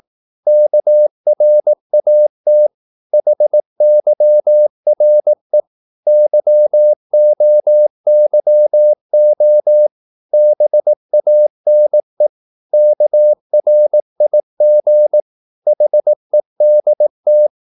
Morse code, heard as a keyed tone, about 18 wpm, 600 Hz. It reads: KRAT HYRE YOYO BANE KRIG HEDT
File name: Korte ord DK 18wpm